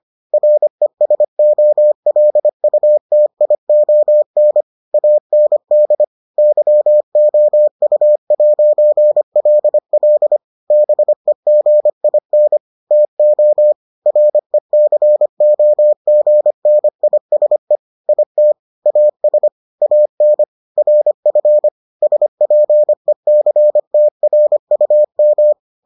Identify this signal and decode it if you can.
Morse code, heard as a keyed tone, about 25 wpm, 600 Hz. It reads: RESOLUTION AND YOU'LL BEGIN TO RECOGNISE IT AS AN RF SPECTRUM